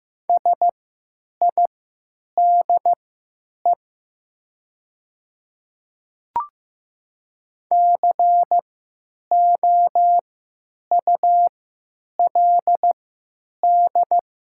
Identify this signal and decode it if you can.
Morse code, heard as a keyed tone, about 15 wpm, spaced as for 9 wpm, 700 Hz. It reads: SIDE COULD
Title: 3x Character Spacing - Top 100 Words - 15wpm